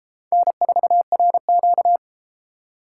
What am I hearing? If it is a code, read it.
Morse code, heard as a keyed tone, about 31 wpm, 700 Hz. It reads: N4RQ